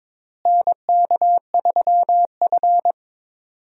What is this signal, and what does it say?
Morse code, heard as a keyed tone, about 22 wpm, 700 Hz. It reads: NK3F
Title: USA Call Signs - 4 Characters - Repeated 3x - 22wpm